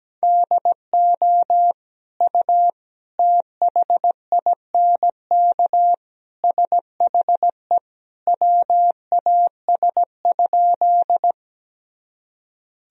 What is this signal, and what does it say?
Morse code, heard as a keyed tone, about 17 wpm, 700 Hz. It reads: DO U THINK SHE WAS?
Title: Sentences from Top 200 Words and Abbreviations 17wpm